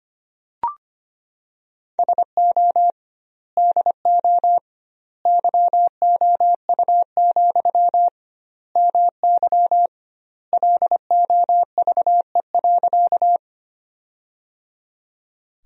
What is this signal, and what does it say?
Morse code, heard as a keyed tone, about 25 wpm, 700 Hz. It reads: SO DO YOU, MY LOVE.
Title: 2x Word-Spacing - Sentences from Top 400 Words - 25wpm